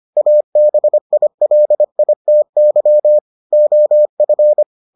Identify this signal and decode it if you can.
Morse code, heard as a keyed tone, about 25 wpm, 600 Hz. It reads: ABILITY OF